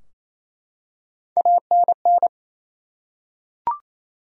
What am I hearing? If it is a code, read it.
Morse code, heard as a keyed tone, about 28 wpm, 700 Hz. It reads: ANN
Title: USA Common Names - Max 3 Letters - 28wpm